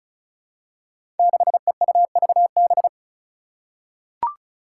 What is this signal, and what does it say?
Morse code, heard as a keyed tone, about 36 wpm, 700 Hz. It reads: 6EUVB
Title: Five Challenging Letter-Number Combinations 35wpm